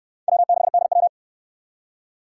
Morse code, heard as a keyed tone, about 57 wpm, 700 Hz. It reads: W6DW